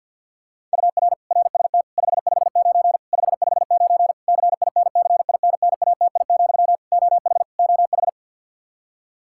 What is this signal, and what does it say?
Morse code, heard as a keyed tone, about 51 wpm, 700 Hz. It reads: UR RST 559 559 CINCINNATI, OH OH
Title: Practice QSO - Exchange 1 - Beginning - 50wpm